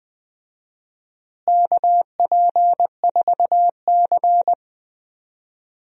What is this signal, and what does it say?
Morse code, heard as a keyed tone, about 19 wpm, 700 Hz. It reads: KP4C
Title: DXCC-100 Call Signs - 4 Characters - Repeated 3x - 20wpm